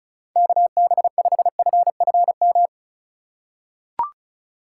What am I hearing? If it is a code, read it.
Morse code, heard as a keyed tone, about 36 wpm, 700 Hz. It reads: KB5FFM
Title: USA Call Signs - 6 Characters - 2x Character Spacing - 35wpm